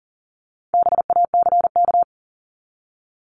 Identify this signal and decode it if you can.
Morse code, heard as a keyed tone, about 40 wpm, 700 Hz. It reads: BACK
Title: Top 200 Words 40wpm